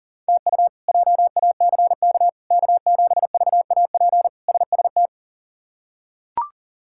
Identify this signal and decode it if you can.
Morse code, heard as a keyed tone, about 40 wpm, 700 Hz. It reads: TU JACK K7VAP SST
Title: K1USN SST QSO Practice - 40wpm